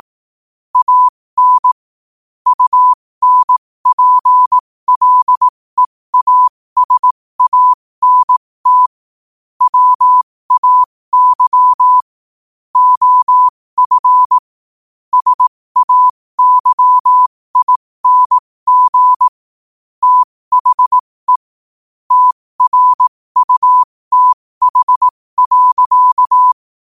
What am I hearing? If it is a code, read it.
Morse code, heard as a keyed tone, about 18 wpm, spaced as for 15 wpm, 1000 Hz. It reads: AN UNPLEASANT WAY OF SAYING THE TRUTH.
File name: qotdfiles / 15wpm